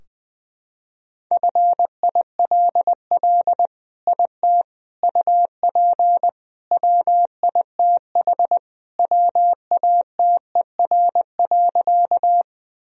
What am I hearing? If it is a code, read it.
Morse code, heard as a keyed tone, about 21 wpm, 700 Hz. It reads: FILL IT UP WITH WATER.
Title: Sentences from Top 1000 Words - Review Part 3 - 20wpm